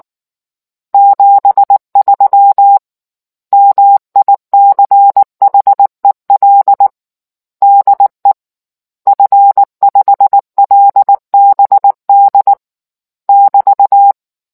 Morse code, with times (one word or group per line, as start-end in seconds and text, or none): 0.9-2.8 73
3.5-6.9 MICHEL
7.6-8.3 DE
9.1-12.5 F5LBD
13.3-14.1 <BT>